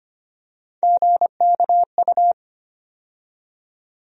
Morse code, heard as a keyed tone, about 25 wpm, 700 Hz. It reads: GKU
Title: USA Call Sign Postfixes - 3 Letters 25wpm